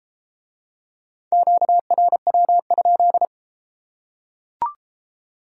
Morse code, heard as a keyed tone, about 30 wpm, 700 Hz. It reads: QRW?